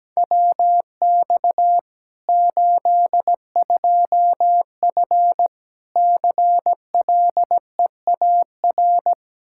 Morse code, heard as a keyed tone, about 17 wpm, 700 Hz. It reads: WX 82F CLEAR
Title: Practice QSO - Exchange 2 - Info - 17wpm